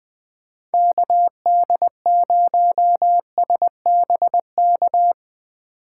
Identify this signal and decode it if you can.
Morse code, heard as a keyed tone, about 20 wpm, 700 Hz. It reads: KD0SBK
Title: USA Call Signs - 6 Characters - 4x Character Spacing - 20wpm